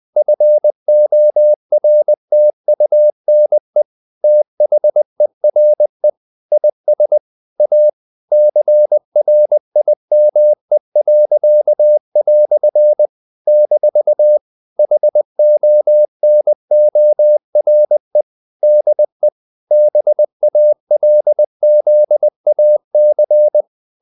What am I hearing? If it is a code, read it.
Morse code, heard as a keyed tone, about 20 wpm, 600 Hz. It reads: FORTUNE THERE IS A CRIME." - HONORE DE BALZAC
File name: CW / Practice / EnglishQuotations0320